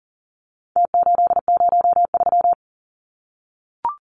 Morse code, heard as a keyed tone, about 40 wpm, 700 Hz. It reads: T803
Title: Numbers - Contest Style - 4-Digit 40wpm